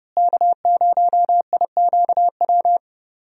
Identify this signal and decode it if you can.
Morse code, heard as a keyed tone, about 30 wpm, 700 Hz. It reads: K0IQW
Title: USA Call Signs - 5 Characters - Repeated 2x - 30wpm